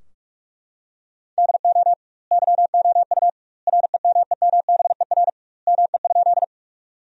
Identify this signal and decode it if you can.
Morse code, heard as a keyed tone, about 41 wpm, 700 Hz. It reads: DO YOU REMEMBER ME?
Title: 2x Word-Spacing - Sentences from Top 400 Words - 45wpm